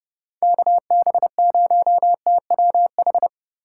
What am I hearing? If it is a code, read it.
Morse code, heard as a keyed tone, about 29 wpm, 700 Hz. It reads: KB0TWH